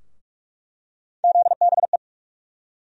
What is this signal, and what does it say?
Morse code, heard as a keyed tone, about 47 wpm, 700 Hz. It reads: ZBE
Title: USA Call Sign Postfixes - 3 Letters 45wpm